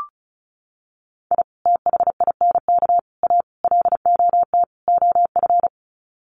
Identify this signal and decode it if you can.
Morse code, heard as a keyed tone, about 34 wpm, 700 Hz. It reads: I THINK A LOT OF
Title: Sets of 5 Words 35wpm